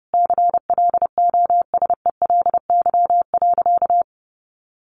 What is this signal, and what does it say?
Morse code, heard as a keyed tone, about 30 wpm, 700 Hz. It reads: CLOSELY.
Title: Sentences from Top 1400 Words 30wpm